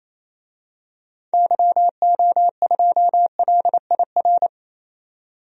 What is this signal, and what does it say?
Morse code, heard as a keyed tone, about 28 wpm, 700 Hz. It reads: YO2LIR
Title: DXCC-100 Call Signs - 6 Characters - 28wpm